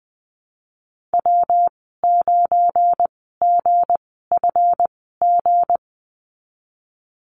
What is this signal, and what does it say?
Morse code, heard as a keyed tone, about 20 wpm, spaced as for 15 wpm, 700 Hz. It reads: W9GFG
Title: 179 Lesson 29g - Slash - Call Signs 20wpm